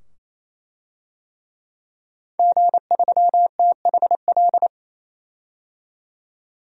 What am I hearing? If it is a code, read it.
Morse code, heard as a keyed tone, about 28 wpm, 700 Hz. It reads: G3THL
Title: DXCC-100 Call Signs - 5 Characters - 28wpm